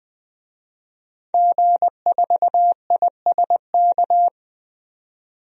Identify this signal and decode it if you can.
Morse code, heard as a keyed tone, about 20 wpm, 700 Hz. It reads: G4ISK
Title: DXCC-100 Call Signs - 5 Characters - 20wpm